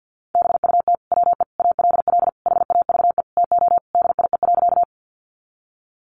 Morse code, heard as a keyed tone, about 50 wpm, 700 Hz. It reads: BUT WE ALL HAVE TO DIE.
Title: Sentences from Top 1300 Words 50wpm